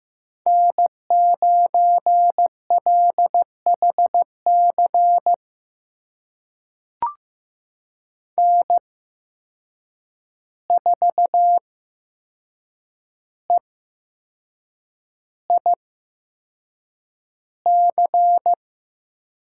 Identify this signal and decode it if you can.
Morse code, heard as a keyed tone, about 15 wpm, spaced as for 6 wpm, 700 Hz. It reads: N9LHC N4EIC
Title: USA Call Signs - 5 Characters - 8x Character Spacing - 15wpm